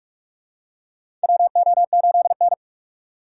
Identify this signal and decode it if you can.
Morse code, heard as a keyed tone, about 44 wpm, 690 Hz. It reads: WO8N
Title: USA Call Signs - 4 Characters - 2x Character Spacing - 45wpm